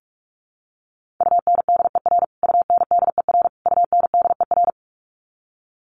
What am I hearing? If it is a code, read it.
Morse code, heard as a keyed tone, about 44 wpm, 700 Hz. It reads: UNDER UNDER UNDER
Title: Top 500 Words Repeated 45wpm